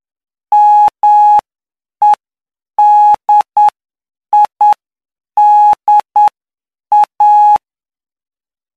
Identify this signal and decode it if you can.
Morse code, heard as a keyed tone, about 9 wpm, spaced as for 7 wpm, 840 Hz. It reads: MEDIDA
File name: cw / py2lcd / 08-PY2LCD-Li��o 17